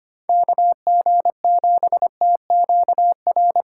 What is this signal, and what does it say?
Morse code, heard as a keyed tone, about 25 wpm, 700 Hz. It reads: KG7TQR
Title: USA Call Signs - 3x Character Spacing - 25wpm